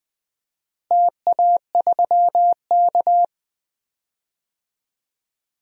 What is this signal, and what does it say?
Morse code, heard as a keyed tone, about 20 wpm, 700 Hz. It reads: TA3K
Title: DXCC-100 Call Signs - 4 Characters - Repeated 3x - 20wpm